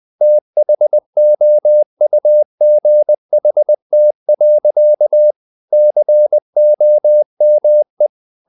Morse code, heard as a keyed tone, about 20 wpm, 600 Hz. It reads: THOUGHT. COME
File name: the old man and the sea / the old man and the sea part 1 20wpm